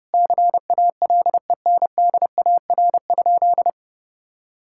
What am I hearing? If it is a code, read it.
Morse code, heard as a keyed tone, about 30 wpm, 700 Hz. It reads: CALENDAR?
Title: Sentences from Top 1600 Words 30wpm